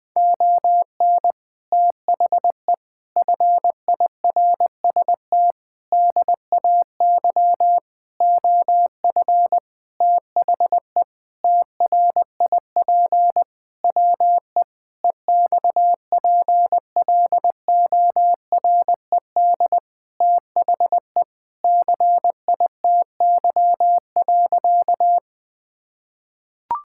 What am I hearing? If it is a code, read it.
Morse code, heard as a keyed tone, about 20 wpm, 700 Hz. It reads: ON THE FIRST DAY OF THE TRIP WE EXPLORED THE CITY.
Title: Sentences with 6-Word Phrases - 20wpm